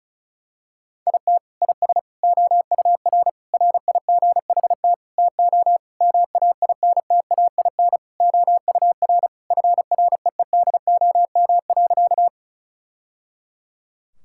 Morse code, heard as a keyed tone, about 34 wpm, 700 Hz. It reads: IT IS OUR RIGHT TO MAINTAIN OUR FREEDOM.
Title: Sentences from Top 1000 Words - Review Part 5 - 35wpm